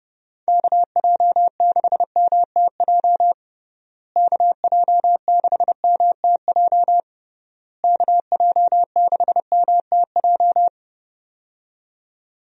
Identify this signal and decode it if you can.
Morse code, heard as a keyed tone, about 32 wpm, 700 Hz. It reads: KJ6MTJ KJ6MTJ KJ6MTJ
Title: USA Call Signs - 6 Characters - Repeated 3x - 30wpm